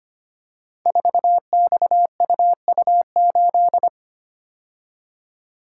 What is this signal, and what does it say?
Morse code, heard as a keyed tone, about 25 wpm, 700 Hz. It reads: 4XUU8
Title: Code Groups - 5 Letter:Numbers - 25wpm